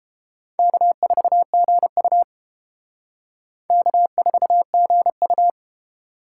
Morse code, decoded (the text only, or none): K4GU K4GU